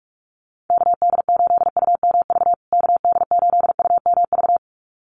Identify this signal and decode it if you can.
Morse code, heard as a keyed tone, about 44 wpm, 700 Hz. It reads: KD8UMV KD8UMV
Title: USA Call Signs Repeated 2x 45wpm